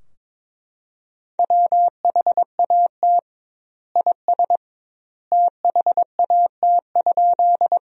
Morse code, heard as a keyed tone, about 22 wpm, 700 Hz. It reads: WHAT IS THAT?
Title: 2x Word-Spacing - Sentences from Top 100 Words - 22wpm